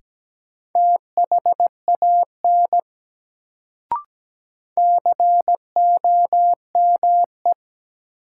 Morse code, heard as a keyed tone, about 17 wpm, 700 Hz. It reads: THAN COME
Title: Sets of 2 Words - New Word Drill - 17wpm